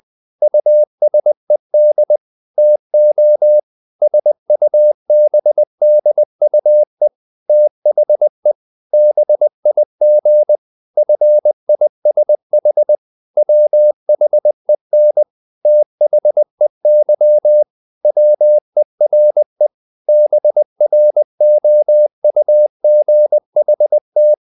Morse code, heard as a keyed tone, about 20 wpm, 600 Hz. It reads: USED TO SUBDUE THE BIG FISH WHEN THEY WERE BROUGHT
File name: the old man and the sea / the old man and the sea part 1 20wpm